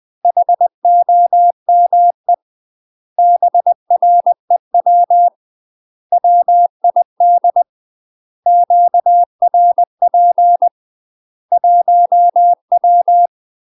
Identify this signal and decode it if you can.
Morse code, heard as a keyed tone, about 20 wpm, 700 Hz. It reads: HOME BREW WID QRP 1W
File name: UK MASTER QSOs TRAINING / 20WPM / UK MASTER QSOs 0068